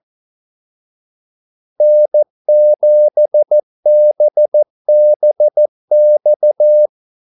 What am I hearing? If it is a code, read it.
Morse code, heard as a keyed tone, about 14 wpm, 600 Hz. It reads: N7BBX
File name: Callsigns-14wpm